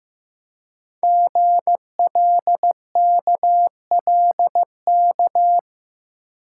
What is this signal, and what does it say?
Morse code, heard as a keyed tone, about 15 wpm, 690 Hz. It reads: GLKLK
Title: Code Groups - 5 Letters - 15wpm